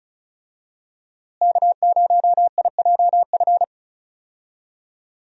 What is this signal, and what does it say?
Morse code, heard as a keyed tone, about 34 wpm, 700 Hz. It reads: K0IJF
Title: USA Call Signs - 5 Characters - Repeated 3x - 35wpm